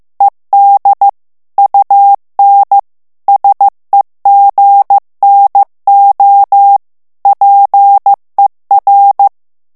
Morse code, heard as a keyed tone, about 15 wpm, 800 Hz. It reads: ED UN SEGNO PER
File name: ITALIEN texte clair 1 vitesse 14 mots minute